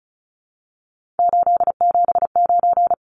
Morse code, heard as a keyed tone, about 34 wpm, 700 Hz. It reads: 879
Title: Numbers - 3-Digit 35wpm